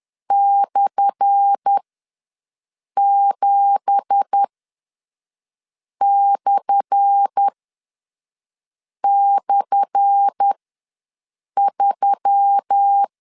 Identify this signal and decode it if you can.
Morse code, heard as a keyed tone, about 10 wpm, spaced as for 5 wpm, 800 Hz. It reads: /7//3
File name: LEZIONI / 10 SIGNS / 10A SLASH ws 10wpm 4121